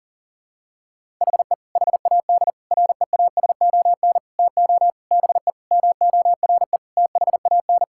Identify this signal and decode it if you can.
Morse code, heard as a keyed tone, about 40 wpm, 700 Hz. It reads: HE HAD REASON TO BE MORE THAN
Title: Sentences from Top 1000 Words - Review Part 6 - 40wpm